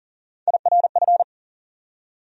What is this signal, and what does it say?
Morse code, heard as a keyed tone, about 40 wpm, 700 Hz. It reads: IRF
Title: USA Call Sign Postfixes - 3 Letters 40wpm